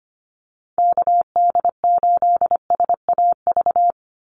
Morse code, heard as a keyed tone, about 25 wpm, 700 Hz. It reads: KD8SAV
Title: Call Signs - 25wpm